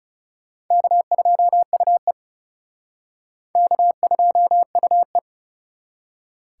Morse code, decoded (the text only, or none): K2UE K2UE